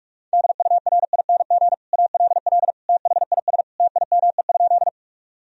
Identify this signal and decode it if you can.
Morse code, heard as a keyed tone, about 42 wpm, 700 Hz. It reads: DURING ALL THIS TIME?
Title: Sentences from Top 1000 Words - Review Part 2 - 45wpm